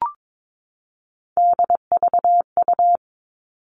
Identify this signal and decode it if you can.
Morse code, heard as a keyed tone, about 22 wpm, 700 Hz. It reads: DVU
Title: All Cut Numbers - 3-Digit 22wpm